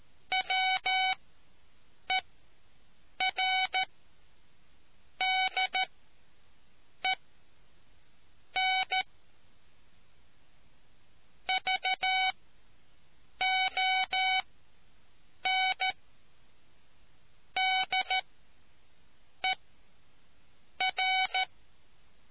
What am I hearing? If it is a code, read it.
Morse code, heard as a keyed tone, about 13 wpm, spaced as for 6 wpm, 750 Hz. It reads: WERDEN VON DER